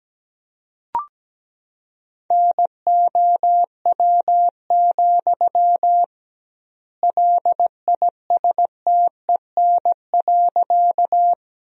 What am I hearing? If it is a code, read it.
Morse code, heard as a keyed tone, about 17 wpm, 700 Hz. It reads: NOW, LISTEN.
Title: 2x Word-Spacing - Sentences from Top 400 Words - 17wpm